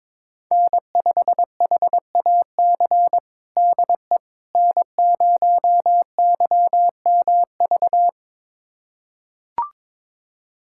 Morse code, heard as a keyed tone, about 22 wpm, 700 Hz. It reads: N5HAC DE N0YMV
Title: USA Callsign DE Callsign - 5 Characters - 22wpm